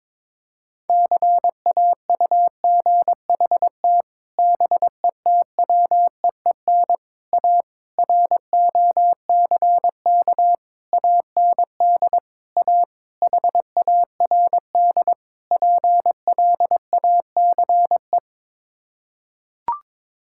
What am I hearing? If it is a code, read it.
Morse code, heard as a keyed tone, about 22 wpm, 700 Hz. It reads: CAUGHT BETWEEN A ROCK AND A HARD PLACE
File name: practice / 22wpm / Sets of 8 Words - 22wpm